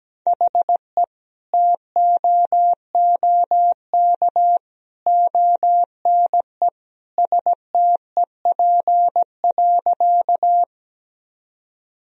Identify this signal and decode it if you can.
Morse code, heard as a keyed tone, about 17 wpm, 700 Hz. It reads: HE TOOK ONE STEP.
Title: Sentences from Top 1000 Words - Review Part 2 - 17wpm